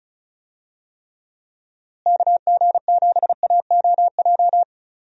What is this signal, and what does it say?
Morse code, heard as a keyed tone, about 36 wpm, 700 Hz. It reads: KG7AOJ
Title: USA Call Signs - 6 Characters - Repeated 3x - 35wpm